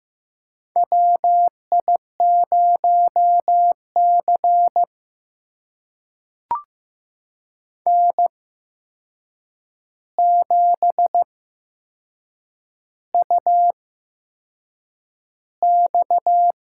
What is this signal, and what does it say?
Morse code, heard as a keyed tone, about 15 wpm, spaced as for 6 wpm, 700 Hz. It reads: WI0C N7UX
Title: USA Call Signs - 4 Characters - 8x Character Spacing - 15wpm